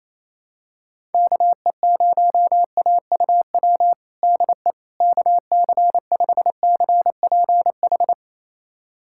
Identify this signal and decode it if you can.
Morse code, heard as a keyed tone, about 28 wpm, 700 Hz. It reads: KE0AUW DE KC5CPH
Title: USA Callsign DE Callsign - 6 Characters - 28wpm